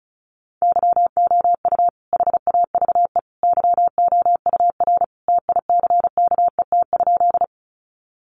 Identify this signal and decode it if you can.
Morse code, heard as a keyed tone, about 34 wpm, 700 Hz. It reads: YOU HAVE YOUR TICKET?
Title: Sentences from Top 1400 Words 35wpm